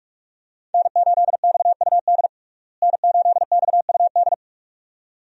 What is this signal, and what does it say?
Morse code, heard as a keyed tone, about 46 wpm, 700 Hz. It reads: N8XUD N8XUD
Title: USA Call Signs - 5 Characters - Repeated 2x - 45wpm